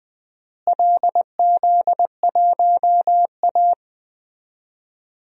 Morse code, heard as a keyed tone, about 20 wpm, 700 Hz. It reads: LZ1A
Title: DXCC-100 Call Signs - 4 Characters - Repeated 3x - 20wpm